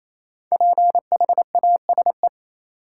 Morse code, heard as a keyed tone, about 28 wpm, 700 Hz. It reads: PHASE
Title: Sentences from Top 1200 Words - New Word Drill - 28wpm